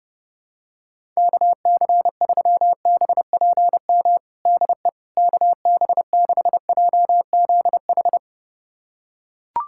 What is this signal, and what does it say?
Morse code, heard as a keyed tone, about 30 wpm, 700 Hz. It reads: KC3BPM DE KB6JZH
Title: USA Callsign DE Callsign - 6 Characters - 30wpm